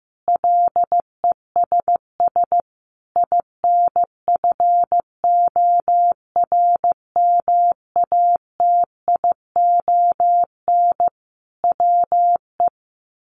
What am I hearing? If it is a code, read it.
Morse code, heard as a keyed tone, about 15 wpm, 700 Hz. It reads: LESS INFORMATION WE
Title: Sentences from Top 1300 Words 15wpm